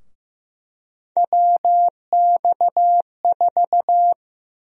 Morse code, heard as a keyed tone, about 15 wpm, 700 Hz. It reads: WX4
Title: USA Call Sign Prefixes - Repeated 2x - 15wpm